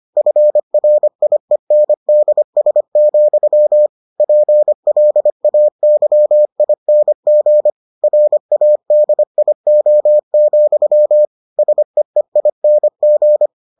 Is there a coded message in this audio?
Yes: Morse code from 0.2 to 13.5 s, heard as a keyed tone, about 25 wpm, 600 Hz. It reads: FRIENDS, PLAYING RADIO, SEEING